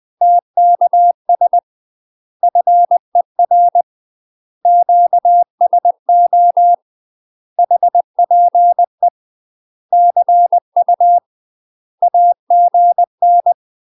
Morse code, heard as a keyed tone, about 20 wpm, 700 Hz. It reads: TKS FER QSO HPE CU AGN
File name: UK MASTER QSOs TRAINING / 20WPM / UK MASTER QSOs 0069